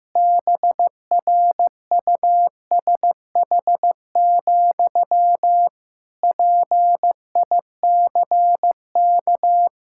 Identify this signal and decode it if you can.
Morse code, heard as a keyed tone, about 15 wpm, 700 Hz. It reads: BRUSH, PICK
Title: Pangram Challenge v2 - 15wpm